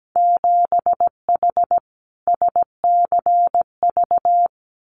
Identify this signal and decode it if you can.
Morse code, heard as a keyed tone, about 17 wpm, 700 Hz. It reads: 7H SCV
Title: Winter Field Day Exchanges Encore 17wpm